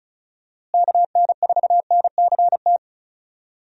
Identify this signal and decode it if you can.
Morse code, heard as a keyed tone, about 36 wpm, 700 Hz. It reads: KN4NCT
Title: USA Call Signs - 6 Characters - Repeated 2x - 35wpm